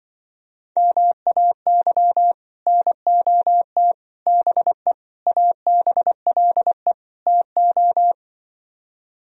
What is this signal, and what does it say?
Morse code, heard as a keyed tone, about 24 wpm, 700 Hz. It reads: MAY NOT BE ABLE TO
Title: Speed-Racing - Sets of 5 Words 20wpm